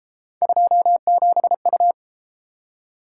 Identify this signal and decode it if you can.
Morse code, heard as a keyed tone, about 32 wpm, 700 Hz. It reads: 27U